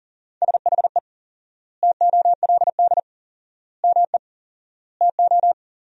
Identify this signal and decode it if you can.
Morse code, heard as a keyed tone, about 40 wpm, 700 Hz. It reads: SHE TOLD ME TO
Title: 4x Word-Spacing - Sentences from Top 400 Words - 40wpm